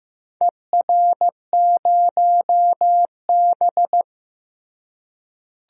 Morse code, heard as a keyed tone, about 15 wpm, 700 Hz. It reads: ER0B
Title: DXCC-100 Call Signs - 4 Characters - Repeated 3x - 15wpm